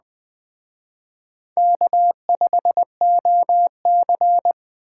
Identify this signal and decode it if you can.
Morse code, heard as a keyed tone, about 20 wpm, 700 Hz. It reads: K5OC